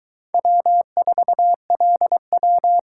Morse code, heard as a keyed tone, about 22 wpm, 700 Hz. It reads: W4LW